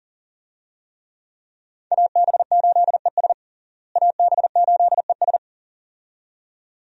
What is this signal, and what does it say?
Morse code, heard as a keyed tone, about 40 wpm, 700 Hz. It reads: AB8ES AB8ES
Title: USA Call Signs - 5 Characters - Repeated 2x - 40wpm